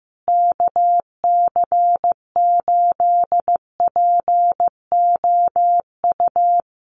Morse code, heard as a keyed tone, about 15 wpm, 700 Hz. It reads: KC8POU
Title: USA Call Signs Repeated 3x 15wpm